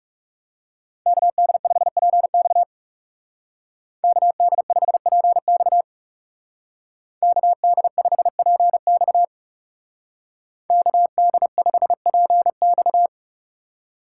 Morse code, heard as a keyed tone, about 38 wpm, 700 Hz. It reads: KD5PX KD5PX KD5PX KD5PX
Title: Speed Racing - USA 2x2 Call Signs 30wpm